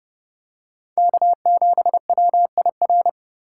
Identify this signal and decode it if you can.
Morse code, heard as a keyed tone, about 27 wpm, 700 Hz. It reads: K7WIR